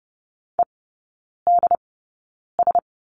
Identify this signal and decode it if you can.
Morse code, heard as a keyed tone, about 30 wpm, spaced as for 9 wpm, 700 Hz. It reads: EDS
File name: Lesson 16e - Letter c - Three Characters 30wpm